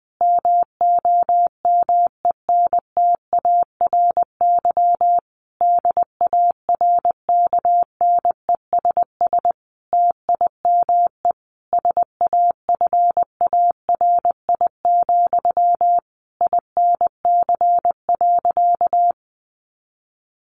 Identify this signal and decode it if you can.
Morse code, heard as a keyed tone, about 20 wpm, 700 Hz. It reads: MOMENTARY DARKNESS TIME SAFARI, INC.
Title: A Sound Of Thunder 20wpm